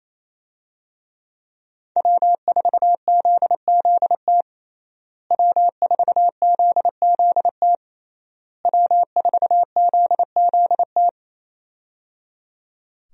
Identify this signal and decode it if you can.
Morse code, heard as a keyed tone, about 28 wpm, 700 Hz. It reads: W4ZZT W4ZZT W4ZZT
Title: USA Call Signs - 5 Characters - Repeated 3x - 28wpm